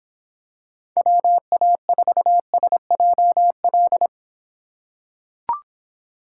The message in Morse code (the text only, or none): WA4SJL